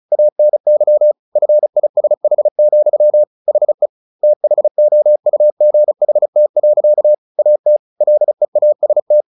Morse code, heard as a keyed tone, about 34 wpm, 600 Hz. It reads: ANY FISH, HE THOUGHT. AT LEAST
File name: the old man and the sea 2 35wpm